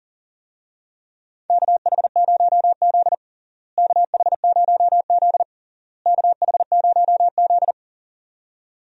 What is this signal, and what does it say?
Morse code, heard as a keyed tone, about 40 wpm, 700 Hz. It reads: KH0Z KH0Z KH0Z
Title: USA Call Signs - 4 Characters - Repeated 3x - 40wpm